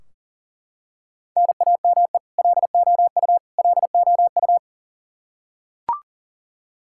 Morse code, heard as a keyed tone, about 40 wpm, 700 Hz. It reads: NAME LOU LOU
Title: Practice QSO - Exchange 1 - Beginning - 40wpm